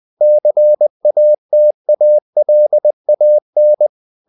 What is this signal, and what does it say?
Morse code, heard as a keyed tone, about 20 wpm, 600 Hz. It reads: CATALAN
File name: the old man and the sea / the old man and the sea part 1 20wpm